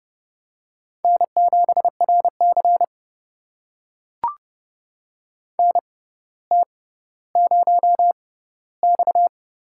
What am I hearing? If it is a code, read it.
Morse code, heard as a keyed tone, about 30 wpm, spaced as for 15 wpm, 700 Hz. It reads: N7RC NT0X
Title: USA Call Signs - 4 Characters - 6x Character Spacing - 30wpm